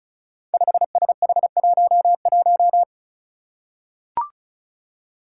Morse code, heard as a keyed tone, about 36 wpm, 700 Hz. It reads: 5SH11
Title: Five Challenging Letter-Number Combinations 35wpm